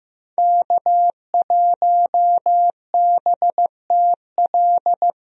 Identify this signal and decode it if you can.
Morse code, heard as a keyed tone, about 15 wpm, 700 Hz. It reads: K1BTL
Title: USA Call Signs - 5 Characters - Repeated 2x - 15wpm